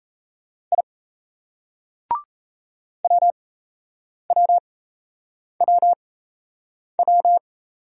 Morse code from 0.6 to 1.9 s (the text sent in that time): I